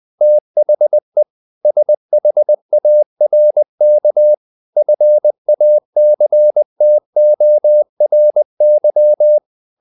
Morse code, heard as a keyed tone, about 20 wpm, 600 Hz. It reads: THE SHARK FACTORY
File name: the old man and the sea / the old man and the sea part 1 20wpm